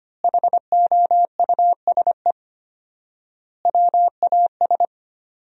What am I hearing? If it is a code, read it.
Morse code, heard as a keyed tone, about 25 wpm, 700 Hz. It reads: HOUSE WAS
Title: 4x Word-Spacing - Sentences from Top 500 Words - 25wpm